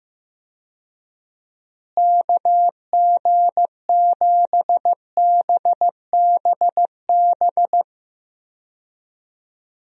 Morse code, heard as a keyed tone, about 15 wpm, 700 Hz. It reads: KG7BBB
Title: USA Call Signs - 6 Characters - Repeated 2x - 15wpm